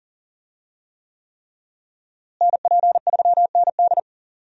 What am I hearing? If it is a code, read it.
Morse code, heard as a keyed tone, about 40 wpm, 700 Hz. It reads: NP3ND
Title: USA Call Signs - 5 Characters - Repeated 3x - 40wpm